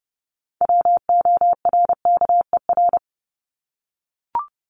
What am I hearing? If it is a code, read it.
Morse code, heard as a keyed tone, about 30 wpm, 700 Hz. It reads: WORKER
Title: ES:S Suffix - Top 100 Words with Base Words 30wpm